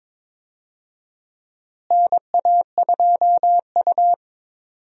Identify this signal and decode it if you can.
Morse code, heard as a keyed tone, about 22 wpm, 700 Hz. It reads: NA2U
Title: USA Call Signs - 4 Characters - Repeated 2x - 22wpm